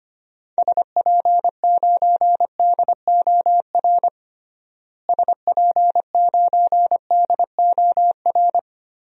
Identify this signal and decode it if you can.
Morse code, heard as a keyed tone, about 25 wpm, 700 Hz. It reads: SP9DOR SP9DOR
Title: DXCC-100 Call Signs - 6 Characters - Repeated 3x - 25wpm